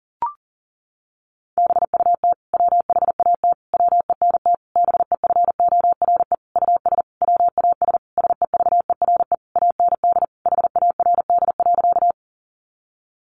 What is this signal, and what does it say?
Morse code, heard as a keyed tone, about 40 wpm, 700 Hz. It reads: BUT WHAT WENT BEFORE US WAS SEVERE AND HARD.
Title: Sentences from Top 1400 Words 40wpm